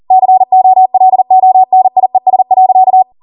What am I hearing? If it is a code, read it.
Morse code, heard as a keyed tone, about 40 wpm, 750 Hz. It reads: COLONIES.